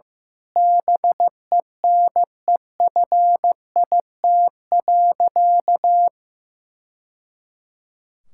Morse code, heard as a keyed tone, about 15 wpm, 700 Hz. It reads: BENEFIT.
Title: Sentences from Top 1000 Words - Review Part 6 - 15wpm